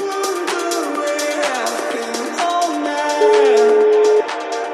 Ringback tone (Europe, 425 Hz), with noise.